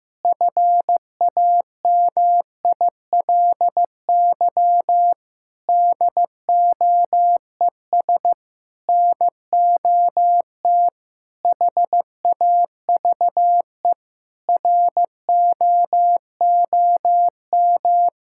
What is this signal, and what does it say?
Morse code, heard as a keyed tone, about 14 wpm, 700 Hz. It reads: FAMILY DOES NOT HAVE ROOM
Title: Sentences from Top 1000 Words - Review Part 5 - 15wpm